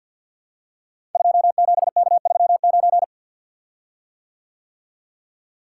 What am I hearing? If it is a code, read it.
Morse code, heard as a keyed tone, about 52 wpm, 700 Hz. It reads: 27K39
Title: Code Groups - 5 Letter:Numbers - 50wpm